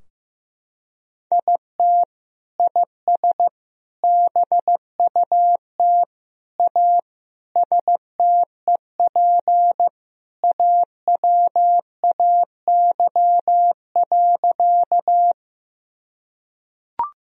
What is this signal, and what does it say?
Morse code, heard as a keyed tone, about 15 wpm, 700 Hz. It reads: IT IS BUT A STEP AWAY.
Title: Sentences from Top 1000 Words - Review Part 2 - 15wpm